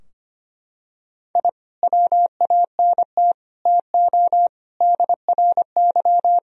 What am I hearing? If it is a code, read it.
Morse code, heard as a keyed tone, about 25 wpm, 700 Hz. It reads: I WANT TO DRY
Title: Sentences from Top 1000 Words - Review Part 3 - 25wpm